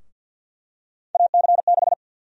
Morse code, heard as a keyed tone, about 52 wpm, 700 Hz. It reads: AC6
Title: USA Call Sign Prefixes - 50wpm